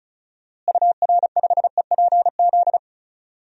Morse code, heard as a keyed tone, about 36 wpm, 700 Hz. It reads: UR5EPZ
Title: DXCC-100 Call Signs - 6 Characters - Repeated 2x - 35wpm